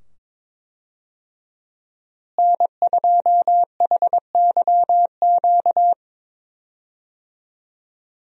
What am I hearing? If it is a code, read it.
Morse code, heard as a keyed tone, about 21 wpm, 700 Hz. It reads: N2HYQ
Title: USA Call Signs - 5 Characters - Repeated 2x - 22wpm